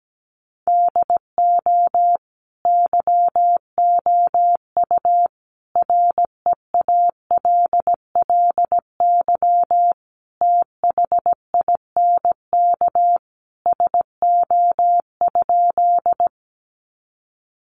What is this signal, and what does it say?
Morse code, heard as a keyed tone, about 17 wpm, 700 Hz. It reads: DO YOU REALLY THINK SO?
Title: Sentences from Top 600 Words 17wpm